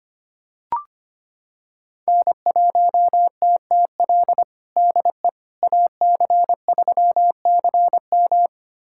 Morse code, heard as a keyed tone, about 25 wpm, 700 Hz. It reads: N1TTL DE AC3CM
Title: USA Callsign DE Callsign - 5 Characters - 25wpm